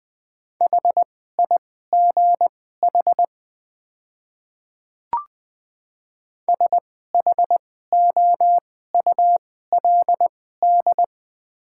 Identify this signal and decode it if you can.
Morse code, heard as a keyed tone, about 20 wpm, spaced as for 15 wpm, 700 Hz. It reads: HIGH SHOULD
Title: 2x Character Spacing - Top 300 Words - 20wpm